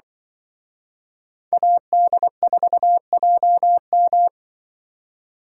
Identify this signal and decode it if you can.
Morse code, heard as a keyed tone, about 23 wpm, 700 Hz. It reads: AD4JM
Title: Speed Racing - USA 2x2 Call Signs 20wpm